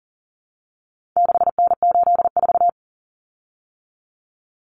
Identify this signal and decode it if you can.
Morse code, heard as a keyed tone, about 40 wpm, 700 Hz. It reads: 6N84
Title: Common Cut Numbers - 4-Digit 40wpm